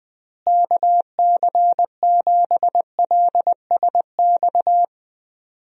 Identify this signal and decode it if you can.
Morse code, heard as a keyed tone, about 21 wpm, 700 Hz. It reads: KC7LSX
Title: USA Call Signs - 6 Characters - Repeated 3x - 20wpm